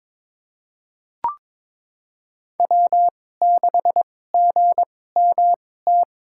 Morse code, heard as a keyed tone, about 23 wpm, spaced as for 16 wpm, 700 Hz. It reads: W6GMT
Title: USA Call Signs - 5 Characters - 2x Character Spacing - 22wpm